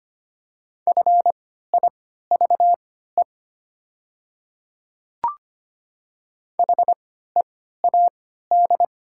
Morse code, heard as a keyed tone, about 25 wpm, spaced as for 14 wpm, 700 Hz. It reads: FIVE HEAD
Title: 3x Character Spacing - Top 500 Words - 25wpm